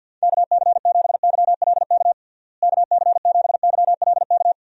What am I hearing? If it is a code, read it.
Morse code, heard as a keyed tone, about 52 wpm, 700 Hz. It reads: KC7YLK KC7YLK
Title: USA Call Signs - 6 Characters - Repeated 2x - 50wpm